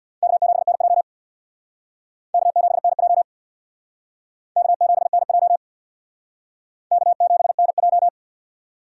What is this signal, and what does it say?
Morse code, heard as a keyed tone, about 64 wpm, 700 Hz. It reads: K7NJ K7NJ K7NJ K7NJ